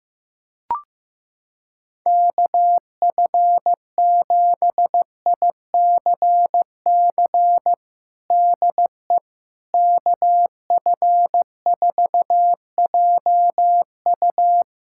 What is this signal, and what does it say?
Morse code, heard as a keyed tone, about 15 wpm, 700 Hz. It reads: KF7ICC DE KF4JU
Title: USA Callsign DE Callsign 15wpm